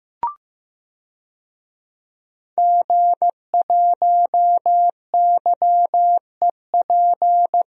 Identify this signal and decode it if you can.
Morse code, heard as a keyed tone, about 15 wpm, 700 Hz. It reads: G1YEP
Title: DXCC-100 Call Signs - 5 Characters - Repeated 2x - 15wpm